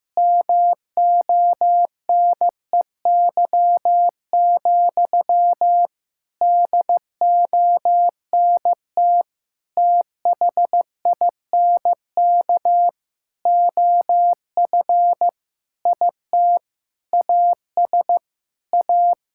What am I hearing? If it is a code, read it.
Morse code, heard as a keyed tone, about 16 wpm, 700 Hz. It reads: MONEY, DONT THINK OF IT AS A
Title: Sentences from Top 1000 Words - Review Part 5 - 15wpm